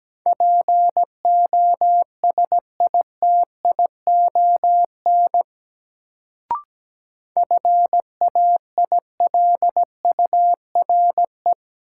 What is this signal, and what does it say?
Morse code, heard as a keyed tone, about 17 wpm, 700 Hz. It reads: POSITION FAILURE
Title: Sentences from Top 900 Words - New Word Drill - 17wpm